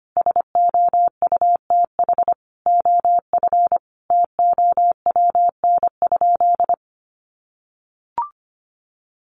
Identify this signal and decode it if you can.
Morse code, heard as a keyed tone, about 25 wpm, 700 Hz. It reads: SOUTH OF TOWN?
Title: Sentences from Top 1300 Words 25wpm